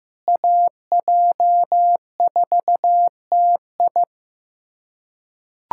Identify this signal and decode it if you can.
Morse code, heard as a keyed tone, about 15 wpm, 700 Hz. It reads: AJ4TI
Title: USA Callsign DE Callsign - 5 Characters - 15wpm